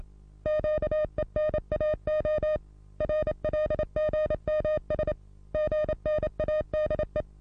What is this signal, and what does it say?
Morse code, heard as a keyed tone, about 28 wpm, 610 Hz. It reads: QENAO RLGMS GNADE
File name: repos / Morsekurs / DJ4UF Morsen27